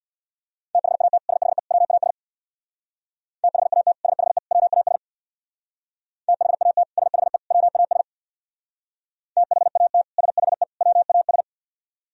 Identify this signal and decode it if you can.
Morse code, heard as a keyed tone, about 61 wpm, 700 Hz. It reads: THAT SHE WAS THAT SHE WAS THAT SHE WAS THAT SHE WAS
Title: Speed-Racing - Sets of 3 Words 50wpm